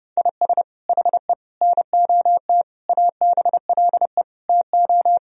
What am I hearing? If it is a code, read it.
Morse code, heard as a keyed tone, about 30 wpm, 700 Hz. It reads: IS HE NOT ABLE TO
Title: Sentences from Top 1000 Words - Review Part 2 - 30wpm